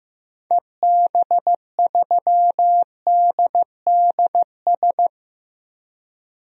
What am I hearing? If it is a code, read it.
Morse code, heard as a keyed tone, about 15 wpm, 700 Hz. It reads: EB3DDS
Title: DXCC-100 Call Signs - 6 Characters - Repeated 2x - 15wpm